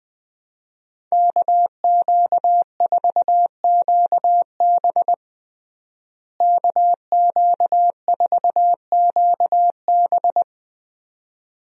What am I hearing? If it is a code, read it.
Morse code, heard as a keyed tone, about 20 wpm, 700 Hz. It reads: KQ4QB KQ4QB
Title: USA Call Signs - 5 Characters - Repeated 2x - 20wpm